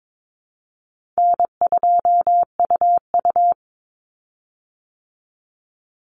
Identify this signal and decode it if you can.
Morse code, heard as a keyed tone, about 22 wpm, 700 Hz. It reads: N2UU
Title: Call Signs Repeated 3x - 22wpm